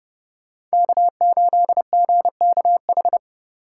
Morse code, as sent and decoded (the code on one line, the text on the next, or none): -.- ---.. --. -.- ....
K8GKH